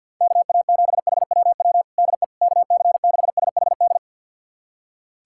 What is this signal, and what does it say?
Morse code, heard as a keyed tone, about 50 wpm, 700 Hz. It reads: KA7HWW DE KC6SHD
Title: USA Callsign DE Callsign 50wpm